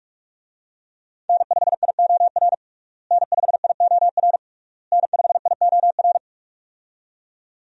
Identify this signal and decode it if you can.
Morse code, heard as a keyed tone, about 46 wpm, 700 Hz. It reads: N5IOR N5IOR N5IOR
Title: USA Call Signs - 5 Characters - Repeated 3x - 45wpm